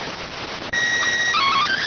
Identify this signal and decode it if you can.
SSTV (slow-scan television), PD 290.